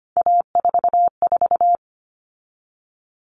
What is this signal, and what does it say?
Morse code, heard as a keyed tone, about 25 wpm, 700 Hz. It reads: A44
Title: All Cut Numbers - 3-Digit 25wpm